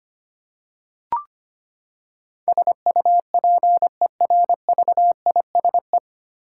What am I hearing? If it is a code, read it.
Morse code, heard as a keyed tone, about 25 wpm, 700 Hz. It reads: SUPERVISE
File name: Nine Letter Words - 25wpm